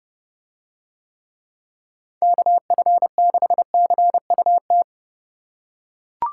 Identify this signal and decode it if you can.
Morse code, heard as a keyed tone, about 31 wpm, 700 Hz. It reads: KF6CUT